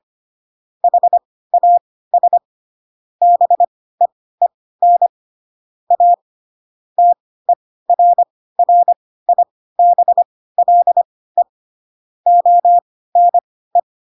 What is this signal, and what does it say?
Morse code, heard as a keyed tone, about 25 wpm, spaced as for 16 wpm, 700 Hz. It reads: HAS BEEN A TERRIBLE ONE
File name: Pinocchio / Pinocchio ch7 10 25 ik1hge